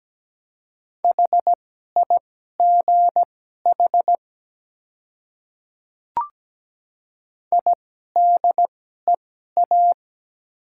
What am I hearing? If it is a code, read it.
Morse code, heard as a keyed tone, about 17 wpm, spaced as for 12 wpm, 700 Hz. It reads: HIGH IDEA
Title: 2x Character Spacing - Top 300 Words - 17wpm